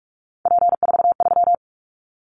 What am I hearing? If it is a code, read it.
Morse code, heard as a keyed tone, about 46 wpm, 700 Hz. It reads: P43
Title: Callsign Prefixes for Top 100 DXCC - 45wpm